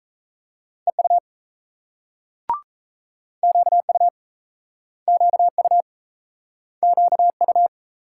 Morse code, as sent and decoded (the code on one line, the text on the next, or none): . ..- / --.- ..- / --.- ..- / --.- ..-
EU QU QU QU